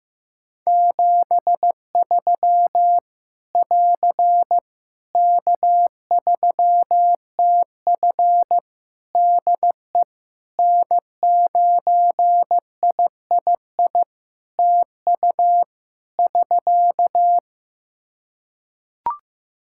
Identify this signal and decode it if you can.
Morse code, heard as a keyed tone, about 15 wpm, 700 Hz. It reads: 73 <AR> K3TF DE N9III TU <SK>
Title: Practice QSO - Exchange 4 - Ending - 15wpm